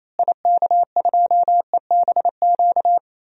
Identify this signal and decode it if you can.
Morse code, heard as a keyed tone, about 28 wpm, 700 Hz. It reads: IK2EBQ
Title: DXCC-100 Call Signs - 6 Characters - Repeated 3x - 28wpm